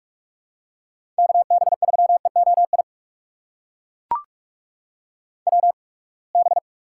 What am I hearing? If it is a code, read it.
Morse code, heard as a keyed tone, about 42 wpm, spaced as for 30 wpm, 700 Hz. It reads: KB3EOI WB
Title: USA Call Signs - 6 Characters - 8x Character Spacing - 45wpm